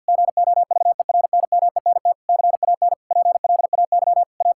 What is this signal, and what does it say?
Morse code, heard as a keyed tone, about 49 wpm, 700 Hz. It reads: GOVERNMENT CAN PLAY A
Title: Sentences from Top 1000 Words - Review Part 5 - 50wpm